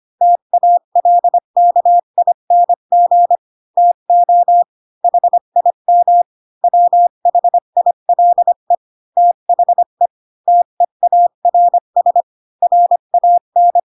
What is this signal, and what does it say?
Morse code, heard as a keyed tone, about 25 wpm, 700 Hz. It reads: TALKING TO HIM WHILE THE TEARS RAN
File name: Pinocchio / Pinocchio ch7 20 25 ik1hge